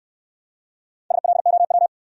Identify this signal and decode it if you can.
Morse code, heard as a keyed tone, about 63 wpm, 700 Hz. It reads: SLOW